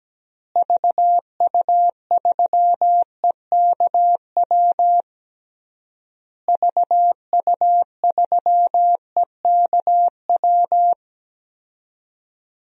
Morse code, heard as a keyed tone, about 17 wpm, 700 Hz. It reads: VU3EKW VU3EKW
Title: DXCC-100 Call Signs - 6 Characters - Repeated 3x - 17wpm